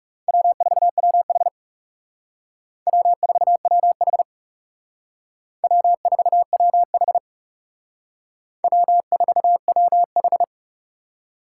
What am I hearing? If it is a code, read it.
Morse code, heard as a keyed tone, about 37 wpm, 700 Hz. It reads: W4WH W4WH W4WH W4WH
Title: Speed Racing - USA 1x2 Call Signs 30wpm